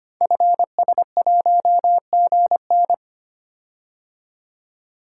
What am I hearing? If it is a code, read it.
Morse code, heard as a keyed tone, about 25 wpm, 700 Hz. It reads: FS1GN
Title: Code Groups - 5 Letter:Numbers - 25wpm